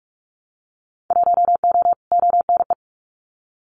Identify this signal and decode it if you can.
Morse code, heard as a keyed tone, about 46 wpm, 700 Hz. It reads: 1O ONE
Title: Winter Field Day Exchanges 45wpm